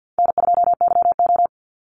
Morse code, heard as a keyed tone, about 48 wpm, 700 Hz. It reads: N2YO
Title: Easy Call Signs Repeated 3x - 50wpm